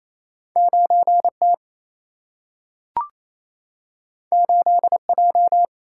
Morse code, heard as a keyed tone, about 28 wpm, 700 Hz. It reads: 9T 8J
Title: Single-Letter Single-Number 28wpm